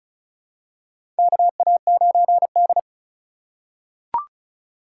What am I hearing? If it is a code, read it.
Morse code, heard as a keyed tone, about 36 wpm, 700 Hz. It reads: KA9D
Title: USA Call Signs - 4 Characters - 4x Character Spacing - 35wpm